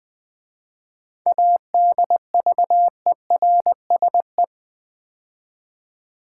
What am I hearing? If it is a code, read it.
Morse code, heard as a keyed tone, about 21 wpm, 700 Hz. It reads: ADVERSE
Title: 7 Letter Words - First 3 Letters Given - 20wpm